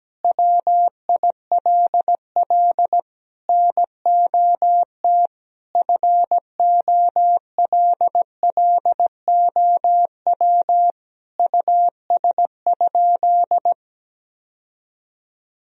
Morse code, heard as a keyed tone, about 17 wpm, 700 Hz. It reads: WILL NOT FOLLOW US?
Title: Sentences from Top 1000 Words - Review Part 1 - 17wpm